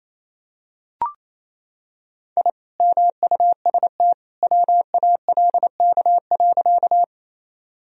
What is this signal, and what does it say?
Morse code, heard as a keyed tone, about 28 wpm, 700 Hz. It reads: I MUST WALK.
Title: Sentences from Top 1000 Words - Review Part 1 - 28wpm